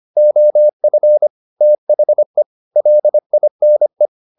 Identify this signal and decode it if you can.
Morse code, heard as a keyed tone, about 25 wpm, 600 Hz. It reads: OF THE LINE
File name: the old man and the sea part 1 25wpm